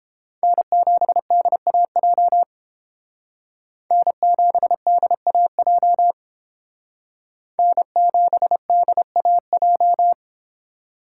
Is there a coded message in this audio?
Yes: Morse code from 0.4 to 10.1 s, heard as a keyed tone, about 30 wpm, 700 Hz. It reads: N7DAJ N7DAJ N7DAJ